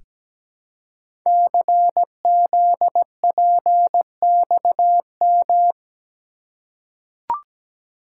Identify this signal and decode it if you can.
Morse code, heard as a keyed tone, about 17 wpm, 700 Hz. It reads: CZPXM
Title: Code Groups - 5 Letters - 17wpm